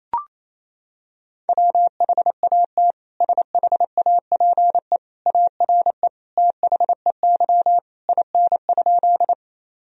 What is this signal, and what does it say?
Morse code, heard as a keyed tone, about 28 wpm, 700 Hz. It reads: WHAT SHAPE ARE THEY IN?
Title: Sentences from Top 1000 Words - Review Part 3 - 28wpm